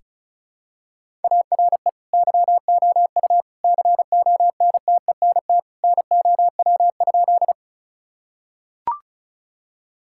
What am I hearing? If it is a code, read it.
Morse code, heard as a keyed tone, about 34 wpm, 700 Hz. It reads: ARE YOU CONTENT NOW?
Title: Sentences from Top 1000 Words - Review Part 6 - 35wpm